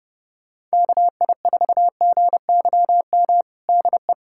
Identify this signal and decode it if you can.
Morse code, heard as a keyed tone, about 30 wpm, 700 Hz. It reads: KI4GYM DE
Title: USA Callsign DE Callsign - 6 Characters - 30wpm